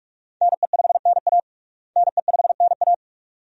Morse code, heard as a keyed tone, about 46 wpm, 700 Hz. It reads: NE5NA NE5NA
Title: USA Call Signs - 5 Characters - Repeated 2x - 45wpm